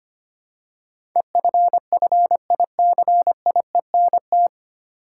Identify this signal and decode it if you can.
Morse code, heard as a keyed tone, about 25 wpm, 700 Hz. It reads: EFFICIENT
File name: Nine Letter Words - 25wpm